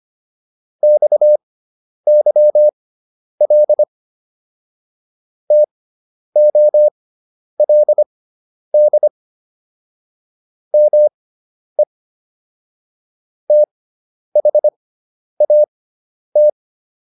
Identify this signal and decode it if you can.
Morse code, heard as a keyed tone, about 25 wpm, spaced as for 10 wpm, 600 Hz. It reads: XYL TOLD ME THAT